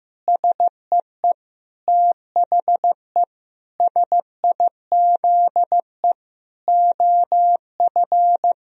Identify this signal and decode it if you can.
Morse code, heard as a keyed tone, about 15 wpm, 700 Hz. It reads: SEE THE SIZE OF
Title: Sentences from Top 1000 Words - Review Part 2 - 15wpm